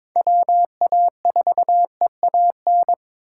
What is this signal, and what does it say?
Morse code, heard as a keyed tone, about 26 wpm, 700 Hz. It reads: WA4EAN
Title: Speed Racing - USA 2x3 Call Signs 22wpm